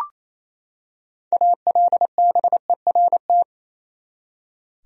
Morse code, heard as a keyed tone, about 28 wpm, 700 Hz. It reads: ALBERT
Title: USA Common Names - 28wpm